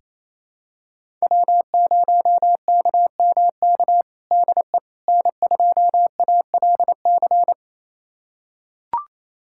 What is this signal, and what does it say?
Morse code, heard as a keyed tone, about 28 wpm, 700 Hz. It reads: W0KMK DE N2ALC
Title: USA Callsign DE Callsign - 5 Characters - 28wpm